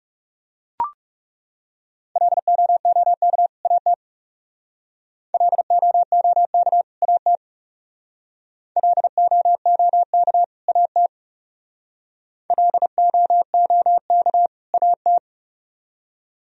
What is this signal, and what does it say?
Morse code, heard as a keyed tone, about 36 wpm, 700 Hz. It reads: LOOK AT LOOK AT LOOK AT LOOK AT
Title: Speed-Racing - Sets of 2 Words 30wpm